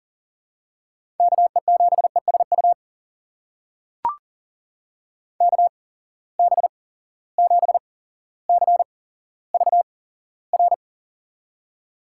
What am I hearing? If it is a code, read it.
Morse code, heard as a keyed tone, about 40 wpm, spaced as for 17 wpm, 700 Hz. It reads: KE7ESU KB7CVR
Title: USA Call Signs - 6 Characters - 8x Character Spacing - 40wpm